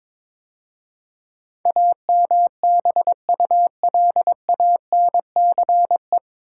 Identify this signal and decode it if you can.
Morse code, heard as a keyed tone, about 22 wpm, 700 Hz. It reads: AMBULANCE
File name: 9 Letter Words - First 3 Letters Given - 22wpm